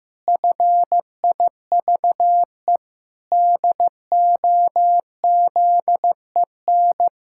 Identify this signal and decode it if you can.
Morse code, heard as a keyed tone, about 15 wpm, 700 Hz. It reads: FIVE DOZEN
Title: Pangram Challenge v2 - 15wpm